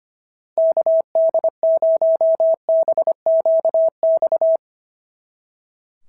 Morse code, heard as a keyed tone, about 25 wpm, 690 Hz. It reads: KD0BQX
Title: SOTA QSO Practice 25wpm